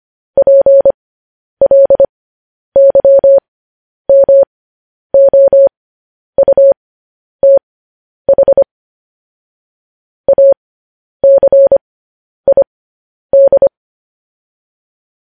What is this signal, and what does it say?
Morse code, heard as a keyed tone, about 25 wpm, spaced as for 10 wpm, 550 Hz. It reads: PLYMOUTH ACID
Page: news-headlines-in-morse-code-at-10-wpm